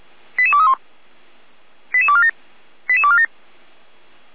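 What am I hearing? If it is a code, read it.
Five-tone selective call (ZVEI), heard as quick runs of five tones, tones of about 70 ms. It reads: 88231 88247 88247
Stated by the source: ZVEI five-tone selective call received via SDR.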